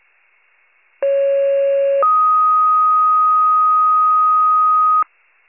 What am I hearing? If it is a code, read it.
Two-tone page, heard as a one-second tone, then a three-second tone, about 550 then 1250 Hz. It is heard from 1.0 to 5.0 s.
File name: wcalert